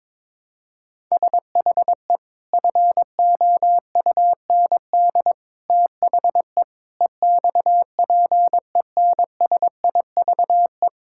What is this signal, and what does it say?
Morse code, heard as a keyed tone, about 22 wpm, 700 Hz. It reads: SHE FOUND THE EXPENSIVE
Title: Sentences from Top 1000 Words - Review Part 6 - 22wpm